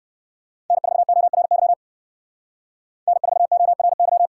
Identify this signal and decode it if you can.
Morse code, heard as a keyed tone, about 68 wpm, 700 Hz. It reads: N4ORY N4ORY